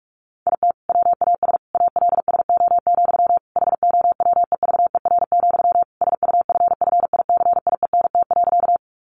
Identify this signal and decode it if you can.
Morse code, heard as a keyed tone, about 44 wpm, 700 Hz. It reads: IT WAS ALSO, HOWEVER, SUFFICIENT.
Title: Sentences from Top 1400 Words 45wpm